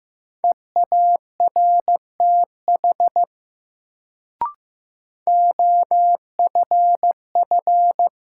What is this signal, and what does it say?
Morse code, heard as a keyed tone, about 15 wpm, 700 Hz. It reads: EARTH OFF
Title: Sentences from Top 200 Words - New Word Drill - 15wpm